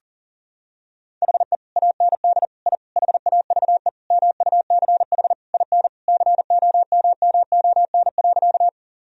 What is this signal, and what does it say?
Morse code, heard as a keyed tone, about 40 wpm, 700 Hz. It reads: HE AND I HAVE MUCH IN COMMON.
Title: Sentences from Top 1000 Words - Review Part 3 - 40wpm